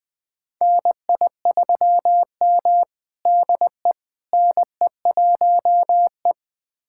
Morse code, heard as a keyed tone, about 20 wpm, 700 Hz. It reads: NI3M DE NE1E
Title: USA Callsign DE Callsign - 4 Characters - 20wpm